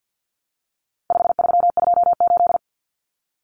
Morse code, heard as a keyed tone, about 48 wpm, 700 Hz. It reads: 5328
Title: Numbers - 4-Digit 50wpm